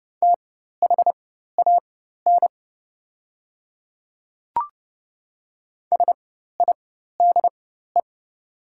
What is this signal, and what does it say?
Morse code, heard as a keyed tone, about 30 wpm, spaced as for 14 wpm, 700 Hz. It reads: THAN SIDE
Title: 4x Character Spacing - Top 200 Words - 30wpm